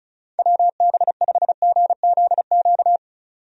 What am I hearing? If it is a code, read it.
Morse code, heard as a keyed tone, about 36 wpm, 700 Hz. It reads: WB5GZQ
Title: USA Call Signs - 6 Characters - 35wpm